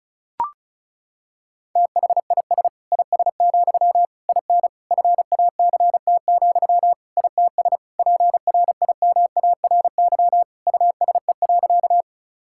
Morse code, heard as a keyed tone, about 36 wpm, 700 Hz. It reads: THIS IS, IN FACT, ITS PRIMARY USE.
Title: Sentences from Top 1000 Words - Review Part 5 - 35wpm